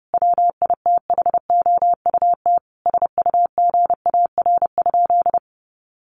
Morse code, heard as a keyed tone, about 30 wpm, 700 Hz. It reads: WITHOUT SUGAR?
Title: Sentences from Top 1200 Words 30wpm